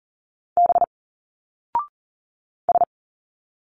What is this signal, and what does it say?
Morse code, heard as a keyed tone, about 39 wpm, 700 Hz. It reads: B S
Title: Single Letters 40wpm